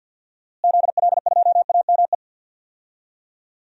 Morse code, heard as a keyed tone, about 48 wpm, 700 Hz. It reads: ZL2AME
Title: DXCC-100 Call Signs - 6 Characters - Repeated 2x - 50wpm